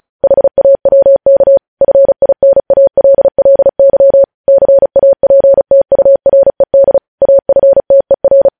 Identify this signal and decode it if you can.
Morse code, heard as a keyed tone, about 36 wpm, 550 Hz. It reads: HAWK FINALLY CAPTURED AFTER
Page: news headlines in morse code at 35 wpm